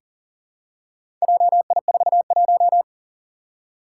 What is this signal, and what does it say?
Morse code, heard as a keyed tone, about 37 wpm, 700 Hz. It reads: JI41
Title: Four Challenging Letter-Number Combinations 40wpm